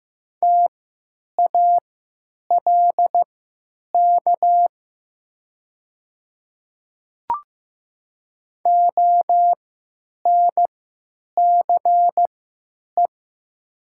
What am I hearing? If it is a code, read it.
Morse code, heard as a keyed tone, about 15 wpm, spaced as for 9 wpm, 700 Hz. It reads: TALK ONCE
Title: 3x Character Spacing - Top 500 Words - 15wpm